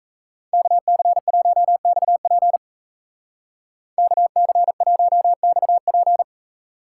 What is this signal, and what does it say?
Morse code, heard as a keyed tone, about 34 wpm, 700 Hz. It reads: KC1XP KC1XP
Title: Speed Racing - USA 2x2 Call Signs 28wpm